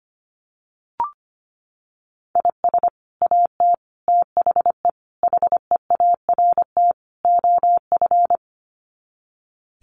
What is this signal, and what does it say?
Morse code, heard as a keyed tone, about 25 wpm, 700 Hz. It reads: IS AT THE HEART OF
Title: Sets of 5 Words Encore2 25wpm